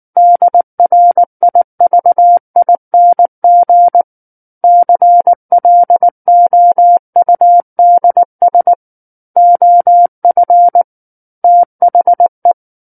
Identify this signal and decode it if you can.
Morse code, heard as a keyed tone, about 19 wpm, 700 Hz. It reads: DRIVING CLOUDS OF THE
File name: Book One / 10-Chapter 10 - 19 WPM